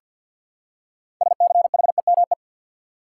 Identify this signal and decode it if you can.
Morse code, heard as a keyed tone, about 48 wpm, 700 Hz. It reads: SCHEME
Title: Six Letter Words - 50wpm